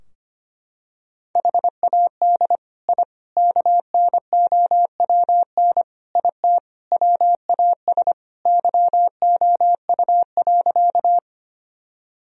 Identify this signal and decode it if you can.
Morse code, heard as a keyed tone, about 25 wpm, 700 Hz. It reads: HAD I KNOWN IT WAS YOU.
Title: Sentences from Top 1000 Words - Review Part 3 - 25wpm